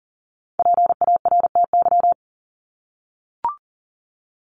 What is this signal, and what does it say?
Morse code, heard as a keyed tone, about 40 wpm, 700 Hz. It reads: PARTY
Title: ES:S Suffix - Top 100 Words with Base Words 40wpm